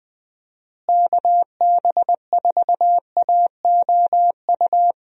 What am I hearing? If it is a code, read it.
Morse code, heard as a keyed tone, about 21 wpm, 700 Hz. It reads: KB4AOU
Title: USA Call Signs - 6 Characters - Repeated 3x - 20wpm